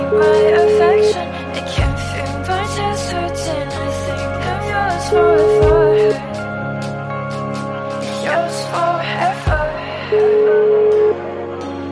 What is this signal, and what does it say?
Ringback tone (Europe, 425 Hz), with speech.